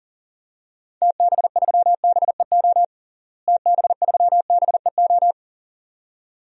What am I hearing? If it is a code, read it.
Morse code, heard as a keyed tone, about 37 wpm, 700 Hz. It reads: TB3BEO TB3BEO
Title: DXCC-100 Call Signs - 6 Characters - Repeated 2x - 40wpm